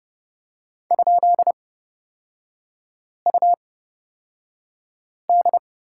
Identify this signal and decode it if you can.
Morse code, heard as a keyed tone, about 30 wpm, spaced as for 5 wpm, 700 Hz. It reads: ?UD
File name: Lesson 36e - BT - Three Characters 30wpm